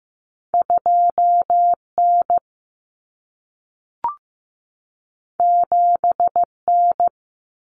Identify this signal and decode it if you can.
Morse code, heard as a keyed tone, about 15 wpm, 700 Hz. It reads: 2N 7N
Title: Common Cut Numbers - 2-Digit 15wpm